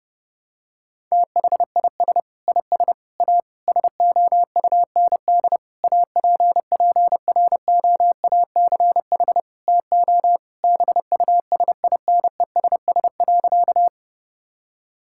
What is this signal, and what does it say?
Morse code, heard as a keyed tone, about 30 wpm, 700 Hz. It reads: THIS IS A SOUND APPROACH TO BUSINESS.
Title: Sentences from Top 1000 Words - Review Part 6 - 30wpm